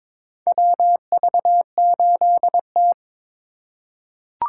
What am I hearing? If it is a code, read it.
Morse code, heard as a keyed tone, about 22 wpm, 700 Hz. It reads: WV8T
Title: USA Call Signs - 4 Characters - 6x Character Spacing - 22wpm